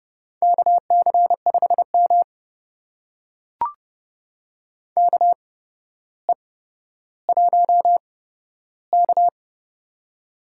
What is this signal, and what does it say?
Morse code, heard as a keyed tone, about 32 wpm, spaced as for 13 wpm, 700 Hz. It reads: KC5M KE1K
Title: USA Call Signs - 4 Characters - 8x Character Spacing - 30wpm